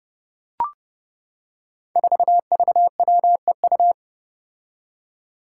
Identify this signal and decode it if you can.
Morse code, heard as a keyed tone, about 30 wpm, 700 Hz. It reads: 4VWEU
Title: Five Challenging Letter-Number Combinations 30wpm